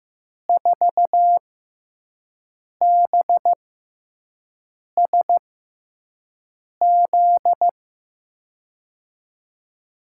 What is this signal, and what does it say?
Morse code, heard as a keyed tone, about 16 wpm, spaced as for 5 wpm, 700 Hz. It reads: 4BSZ
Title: USA Call Signs - 6 Characters - 6x Character Spacing - 15wpm